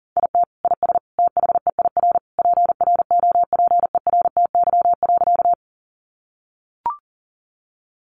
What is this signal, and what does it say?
Morse code, heard as a keyed tone, about 40 wpm, 700 Hz. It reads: IT IS THEIR PROPERTY.
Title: Sentences from Top 1000 Words 40wpm